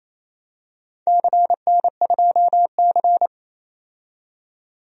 Morse code, heard as a keyed tone, about 28 wpm, 700 Hz. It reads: CN2C
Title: DXCC-100 Call Signs - 4 Characters - 28wpm